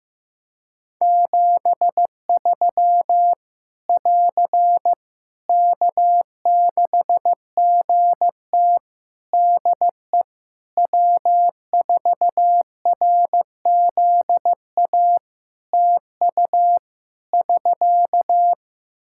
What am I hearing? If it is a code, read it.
Morse code, heard as a keyed tone, about 14 wpm, 700 Hz. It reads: 73 <AR> K6GT DE W4RZA TU <SK>
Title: Practice QSO - Exchange 4 - Ending - 15wpm